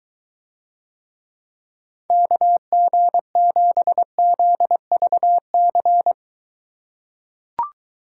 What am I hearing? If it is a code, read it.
Morse code, heard as a keyed tone, about 23 wpm, 700 Hz. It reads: KG7ZVC